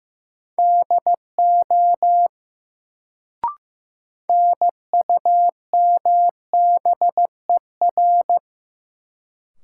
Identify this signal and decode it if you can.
Morse code, heard as a keyed tone, about 15 wpm, 700 Hz. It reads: DO NUMBER
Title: Sentences from Top 100 Words - New Word Drill - 15wpm